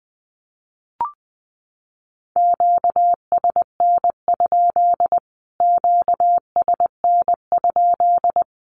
Morse code, heard as a keyed tone, about 20 wpm, 700 Hz. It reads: QSN? QSN?
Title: Q-Codes Repeated 3x 20wpm